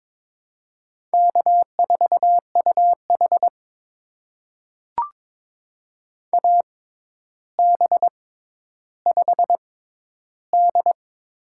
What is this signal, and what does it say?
Morse code, heard as a keyed tone, about 22 wpm, spaced as for 11 wpm, 700 Hz. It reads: K4UH AB5D
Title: USA Call Signs - 4 Characters - 6x Character Spacing - 22wpm